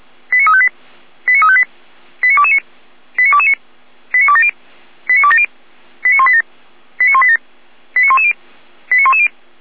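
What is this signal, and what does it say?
Five-tone selective call (ZVEI), heard as quick runs of five tones, tones of about 70 ms. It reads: 89358 89358 89229 89229 89270 89270 89187 89187 89110 89110
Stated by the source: Five-tone ZVEI selective call received on an SDR.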